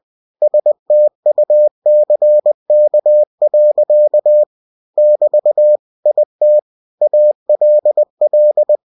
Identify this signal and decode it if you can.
Morse code, heard as a keyed tone, about 20 wpm, 600 Hz. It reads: STUCK. <BT> IT ALL
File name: cw-books / winnie-the-pooh / winnie-20wpm-page-0017